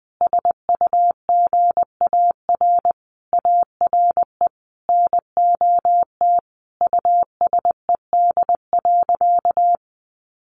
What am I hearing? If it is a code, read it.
Morse code, heard as a keyed tone, about 20 wpm, 700 Hz. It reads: SUGAR ARE NOT USED.
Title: Sentences from Top 1500 Words 20wpm